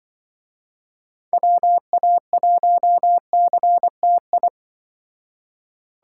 Morse code, heard as a keyed tone, about 24 wpm, 700 Hz. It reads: WA1CTI
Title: Speed Racing - USA Call Signs 20wpm